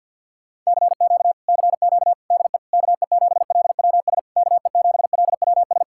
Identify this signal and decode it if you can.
Morse code, heard as a keyed tone, about 61 wpm, 700 Hz. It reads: CQ CQ DE KE7LWS KE7LWS
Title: Speed-Racing Common QSO Phrases - 50wpm